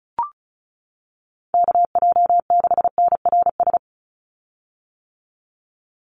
Morse code, heard as a keyed tone, about 34 wpm, 700 Hz. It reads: KJ6NRS
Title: USA Call Signs 35wpm